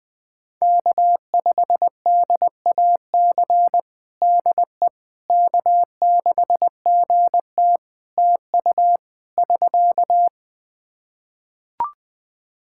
Morse code, heard as a keyed tone, about 21 wpm, 700 Hz. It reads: K5DAC DE K6GT TU <SK>
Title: Practice QSO - Exchange 4 - Ending - 20wpm